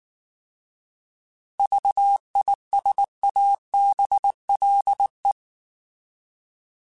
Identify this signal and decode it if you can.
Morse code, heard as a keyed tone, about 19 wpm, 800 Hz. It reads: VISABLE